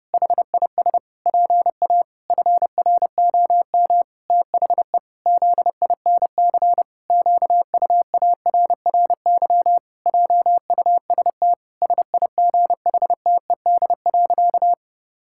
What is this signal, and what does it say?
Morse code, heard as a keyed tone, about 32 wpm, 700 Hz. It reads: HIS PA FROM THE ZINC QUARRY JUST SIGHTED.
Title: Pangram Challenge v2 - 30wpm